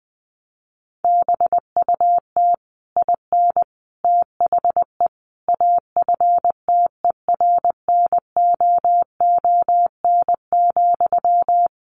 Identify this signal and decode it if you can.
Morse code, heard as a keyed tone, about 20 wpm, 700 Hz. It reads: BUT IN THE AFTERNOON,
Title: Sentences from Top 1500 Words 20wpm